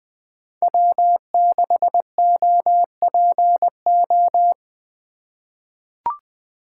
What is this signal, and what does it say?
Morse code, heard as a keyed tone, about 21 wpm, 700 Hz. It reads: W6OPO
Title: K1USN SST QSO Practice - 20wpm